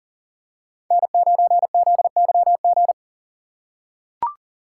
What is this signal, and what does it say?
Morse code, heard as a keyed tone, about 40 wpm, 700 Hz. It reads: N9ZYG
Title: USA Call Signs - 5 Characters - 4x Character Spacing - 40wpm